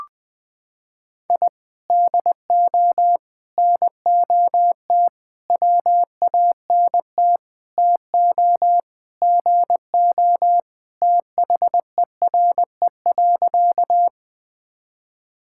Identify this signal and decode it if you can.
Morse code, heard as a keyed tone, about 20 wpm, 700 Hz. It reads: I DO NOT WANT TO GO THERE.
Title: Sentences with 5-Word Phrases - 20wpm